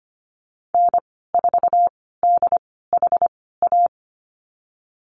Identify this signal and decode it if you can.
Morse code, heard as a keyed tone, about 25 wpm, spaced as for 16 wpm, 700 Hz. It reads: N4DHA
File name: Lesson 15g - Number 5 - Call Signs 25wpm